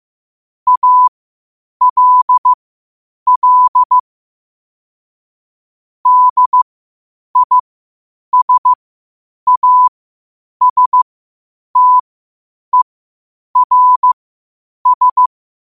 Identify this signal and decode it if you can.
Morse code, heard as a keyed tone, about 15 wpm, spaced as for 8 wpm, 1000 Hz. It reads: ALL DISASTERS
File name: qotdfiles / 8wpm